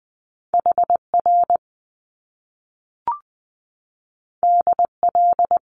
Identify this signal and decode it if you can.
Morse code, heard as a keyed tone, about 20 wpm, 700 Hz. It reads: HR DL
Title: Top Two Letter Combinations in Words 20wpm